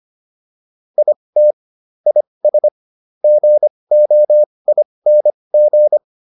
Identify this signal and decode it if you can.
Morse code, heard as a keyed tone, about 25 wpm, spaced as for 20 wpm, 600 Hz. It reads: IT IS GOING
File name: WD405 20